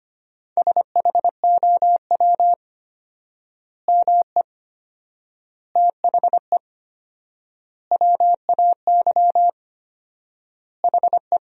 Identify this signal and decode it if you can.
Morse code, heard as a keyed tone, about 25 wpm, 700 Hz. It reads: SHOW ME THE WAY HE
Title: 4x Word-Spacing - Sentences from Top 200 Words - 25wpm